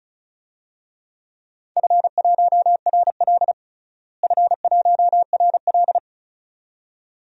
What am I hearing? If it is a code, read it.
Morse code, heard as a keyed tone, about 36 wpm, 700 Hz. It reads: F1RL F1RL
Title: DXCC-100 Call Signs - 4 Characters - Repeated 2x - 35wpm